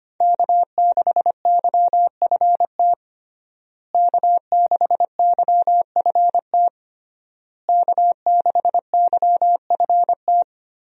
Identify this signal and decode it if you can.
Morse code, heard as a keyed tone, about 25 wpm, 700 Hz. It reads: K6YFT K6YFT K6YFT
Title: USA Call Signs - 5 Characters - Repeated 3x - 25wpm